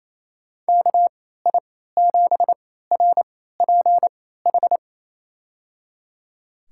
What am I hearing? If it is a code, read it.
Morse code, heard as a keyed tone, about 28 wpm, spaced as for 16 wpm, 700 Hz. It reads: KI7RPH
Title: USA Call Signs - 6 Characters - 3x Character Spacing - 28wpm